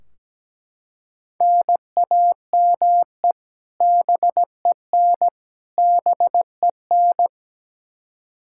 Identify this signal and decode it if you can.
Morse code, heard as a keyed tone, about 17 wpm, 700 Hz. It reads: NAME BEN BEN
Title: Practice QSO - Exchange 1 - Beginning - 17wpm